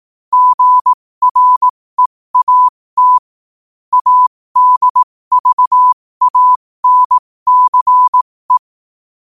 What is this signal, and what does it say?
Morse code, heard as a keyed tone, about 18 wpm, spaced as for 15 wpm, 1000 Hz. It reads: GREAT ADVANCE
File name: qotdfiles / 15wpm